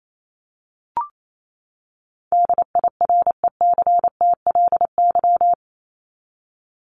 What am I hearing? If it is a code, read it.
Morse code, heard as a keyed tone, about 28 wpm, 700 Hz. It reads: DIRECTLY
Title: LY Suffix - Top 100 Words with Base Words 28wpm